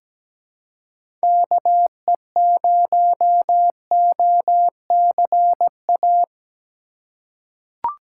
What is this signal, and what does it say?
Morse code, heard as a keyed tone, about 17 wpm, 700 Hz. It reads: KE0OCA
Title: USA Call Signs - 6 Characters - 2x Character Spacing - 17wpm